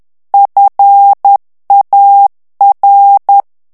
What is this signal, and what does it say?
Morse code, heard as a keyed tone, about 10 wpm, 800 Hz. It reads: FAR
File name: ITALIEN texte clair 2 vitesse 10 mots minute